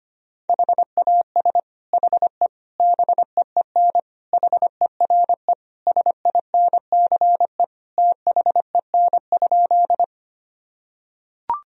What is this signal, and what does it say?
Morse code, heard as a keyed tone, about 25 wpm, 700 Hz. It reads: HAS HE BEEN HERE SINCE THEN?
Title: Sentences from Top 1000 Words - Review Part 1 - 25wpm